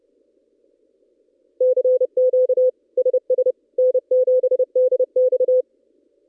Morse code, heard as a keyed tone, about 30 wpm, 510 Hz. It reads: CQ SS N7DX
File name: contests / ss-cw-k5tr-op-n5rz -1700